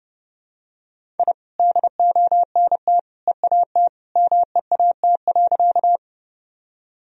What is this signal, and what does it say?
Morse code, heard as a keyed tone, about 31 wpm, 700 Hz. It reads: I DONT EAT MEAT.
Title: Sentences from Top 1000 Words - Review Part 3 - 30wpm